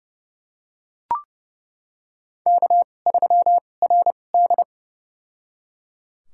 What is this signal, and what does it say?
Morse code, heard as a keyed tone, about 30 wpm, spaced as for 22 wpm, 700 Hz. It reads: K3RD
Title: USA Call Signs - 4 Characters - 2x Character Spacing - 30wpm